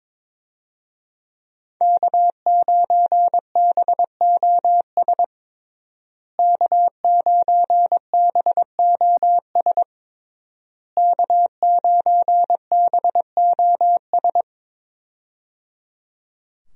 Morse code, heard as a keyed tone, about 22 wpm, 700 Hz. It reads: K9BOS K9BOS K9BOS
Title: USA Call Signs - 5 Characters - Repeated 3x - 22wpm